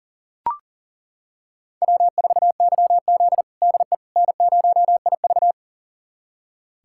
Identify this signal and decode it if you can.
Morse code, heard as a keyed tone, about 40 wpm, 700 Hz. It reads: W4YZ DE N0IV
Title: USA Callsign DE Callsign - 4 Characters - 40wpm